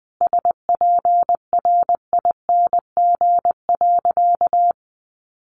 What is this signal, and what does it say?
Morse code, heard as a keyed tone, about 20 wpm, 700 Hz. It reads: SPRING.
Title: Sentences from Top 1200 Words 20wpm